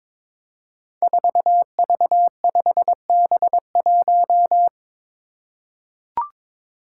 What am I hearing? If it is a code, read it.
Morse code, heard as a keyed tone, about 22 wpm, 700 Hz. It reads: 4V5B1
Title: Five Challenging Letter-Number Combinations 22wpm